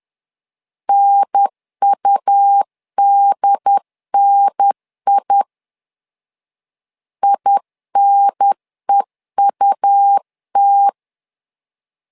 Morse code, heard as a keyed tone, about 10 wpm, 800 Hz. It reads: NUDNI INEUT
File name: LEZIONI / 2 UHDN / 2 UHDN 10wpm 1904